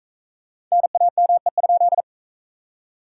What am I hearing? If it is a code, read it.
Morse code, heard as a keyed tone, about 44 wpm, 700 Hz. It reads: NAME?